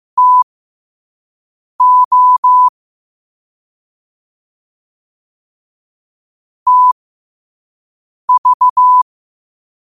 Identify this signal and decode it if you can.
Morse code, heard as a keyed tone, about 15 wpm, spaced as for 5 wpm, 1000 Hz. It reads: TO TV
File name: qotdfiles / 5wpm